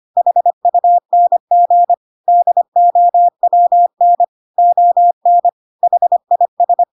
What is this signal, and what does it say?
Morse code, heard as a keyed tone, about 25 wpm, 700 Hz. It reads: HUNG DOWN ON HIS
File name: Pinocchio / Pinocchio ch1 25 25 ik1hge